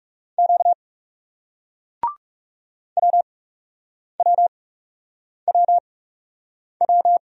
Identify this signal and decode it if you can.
Morse code, heard as a keyed tone, about 40 wpm, spaced as for 8 wpm, 700 Hz. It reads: Q WWWW